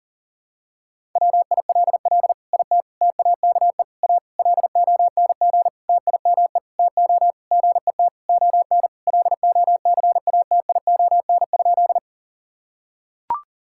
Morse code, heard as a keyed tone, about 40 wpm, 700 Hz. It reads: WILL IT TAKE A LONG TIME TO GET ON LOCATION?
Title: Sentences from Top 1000 Words - Review Part 6 - 40wpm